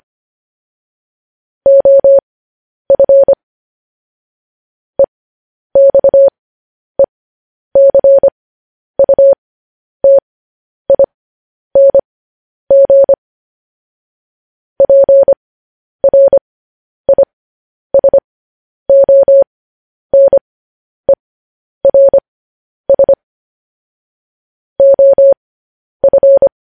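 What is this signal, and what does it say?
Morse code, heard as a keyed tone, about 25 wpm, spaced as for 10 wpm, 550 Hz. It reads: OF EXECUTING PRISONERS OF
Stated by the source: Morse code news headlines practice transmission.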